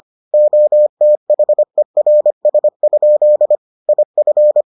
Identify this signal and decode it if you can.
Morse code, heard as a keyed tone, about 25 wpm, 600 Hz. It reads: OTHERS? IF